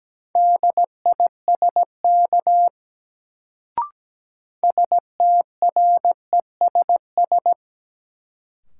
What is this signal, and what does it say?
Morse code, heard as a keyed tone, about 17 wpm, 700 Hz. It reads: DISK STRESS
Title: Sentences from Top 1000 Words - Review Part 6 - New Word Drill - 17wpm